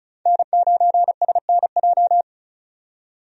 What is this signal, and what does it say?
Morse code, heard as a keyed tone, about 34 wpm, 700 Hz. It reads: N9SNJ
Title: USA Call Signs - 5 Characters - 35wpm